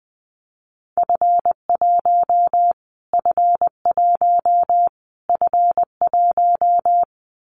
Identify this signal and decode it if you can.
Morse code, heard as a keyed tone, about 20 wpm, 700 Hz. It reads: F1 F1 F1
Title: Callsign Prefixes for Top 25 DXCC - Repeated 3x - 20wpm